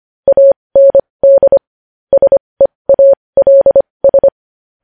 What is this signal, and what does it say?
Morse code, heard as a keyed tone, about 25 wpm, spaced as for 20 wpm, 550 Hz. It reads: AND SEALS
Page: news headlines in morse code at 20 wpm